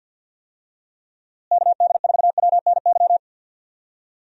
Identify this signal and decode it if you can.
Morse code, heard as a keyed tone, about 51 wpm, 700 Hz. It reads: KD4WNY
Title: USA Call Signs - 6 Characters - Repeated 2x - 50wpm